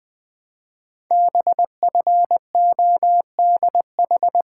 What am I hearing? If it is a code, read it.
Morse code, heard as a keyed tone, about 20 wpm, 700 Hz. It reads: BFODH
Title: Code Groups - 5 Letters - 20wpm